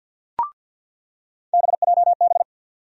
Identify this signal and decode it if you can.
Morse code, heard as a keyed tone, about 48 wpm, 700 Hz. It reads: BJB
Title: USA Call Sign Postfixes - 3 Letters 50wpm